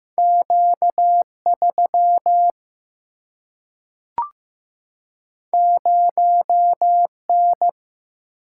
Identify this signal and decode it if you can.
Morse code, heard as a keyed tone, about 15 wpm, 700 Hz. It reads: Q3 0N
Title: Single-Letter Single-Number 15wpm